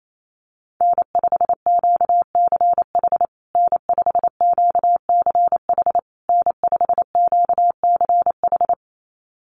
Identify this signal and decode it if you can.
Morse code, heard as a keyed tone, about 28 wpm, 700 Hz. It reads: N5QCH N5QCH N5QCH
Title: USA Call Signs Repeated 3x 28wpm